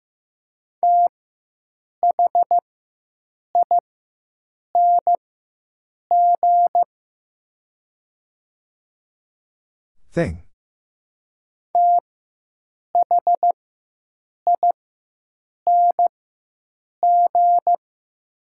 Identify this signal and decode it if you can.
Morse code, heard as a keyed tone, about 15 wpm, spaced as for 7 wpm, 700 Hz. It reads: THING THING
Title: 4x Character Spacing - Top 100 Words - 15wpm